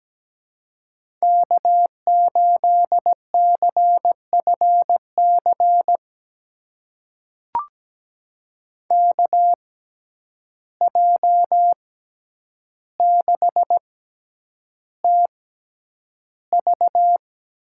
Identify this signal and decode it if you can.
Morse code, heard as a keyed tone, about 17 wpm, spaced as for 9 wpm, 700 Hz. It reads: K8CFC KJ6TV
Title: USA Call Signs - 6x Character Spacing - 17wpm